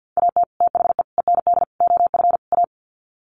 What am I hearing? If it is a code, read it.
Morse code, heard as a keyed tone, about 50 wpm, 700 Hz. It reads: AT THE END OF A
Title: Sets of 5 Words 50wpm